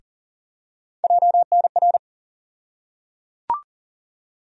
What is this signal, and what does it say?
Morse code, heard as a keyed tone, about 37 wpm, 700 Hz. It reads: JNR
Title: USA Call Sign Postfixes - 3 Letters 40wpm